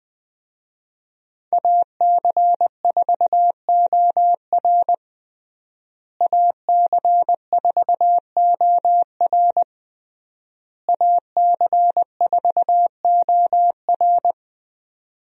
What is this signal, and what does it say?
Morse code, heard as a keyed tone, about 21 wpm, 700 Hz. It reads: AC4OR AC4OR AC4OR
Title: USA Call Signs - 5 Characters - Repeated 3x - 20wpm